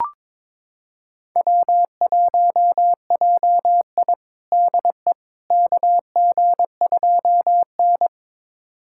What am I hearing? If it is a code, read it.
Morse code, heard as a keyed tone, about 22 wpm, 700 Hz. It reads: W1JI DE KG2N
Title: USA Callsign DE Callsign - 4 Characters - 22wpm